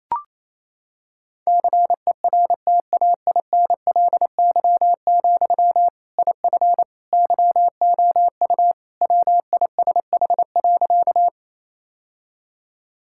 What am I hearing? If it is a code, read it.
Morse code, heard as a keyed tone, about 28 wpm, 700 Hz. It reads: CERTAINLY, IF YOU WISH.
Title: Sentences from Top 1600 Words 28wpm